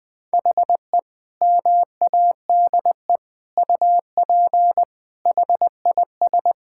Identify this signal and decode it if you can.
Morse code, heard as a keyed tone, about 21 wpm, 700 Hz. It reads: HE MADE UP HIS
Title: Sentences from Top 1600 Words 20wpm